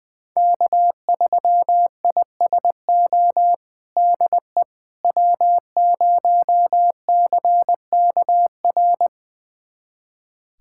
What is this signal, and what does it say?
Morse code, heard as a keyed tone, about 20 wpm, 700 Hz. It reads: K3ISO DE W0CKR
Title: USA Callsign DE Callsign - 5 Characters - 20wpm